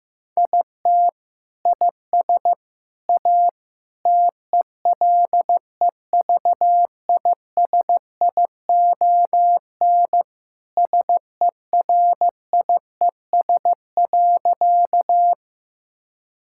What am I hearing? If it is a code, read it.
Morse code, heard as a keyed tone, about 15 wpm, 700 Hz. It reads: IT IS A TELEVISION SERIES.
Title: Sentences from Top 1000 Words - Review Part 5 - 15wpm